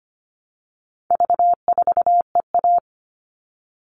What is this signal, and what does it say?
Morse code, heard as a keyed tone, about 25 wpm, 700 Hz. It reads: V4EA
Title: All Cut Numbers - 4-Digit 25wpm